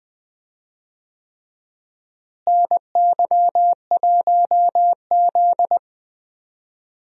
Morse code, heard as a keyed tone, about 20 wpm, 700 Hz. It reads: NY1Z
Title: USA Call Signs - 4 Characters - Repeated 3x - 20wpm